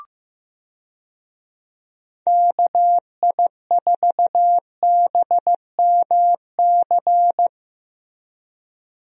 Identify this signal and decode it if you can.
Morse code, heard as a keyed tone, about 15 wpm, 700 Hz. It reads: KI4BMC
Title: USA Call Signs - 6 Characters - Repeated 3x - 15wpm